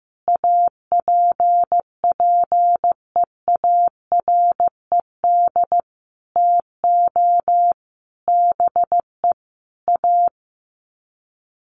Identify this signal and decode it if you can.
Morse code, heard as a keyed tone, about 15 wpm, 700 Hz. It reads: APPEARED TO BE A
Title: Sets of 5 Words Encore2 15wpm